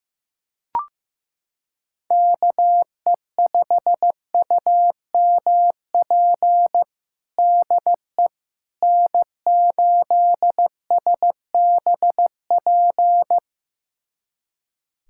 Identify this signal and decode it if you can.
Morse code, heard as a keyed tone, about 15 wpm, 700 Hz. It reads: KE5UMP DE N8SBP
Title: USA Callsign DE Callsign 15wpm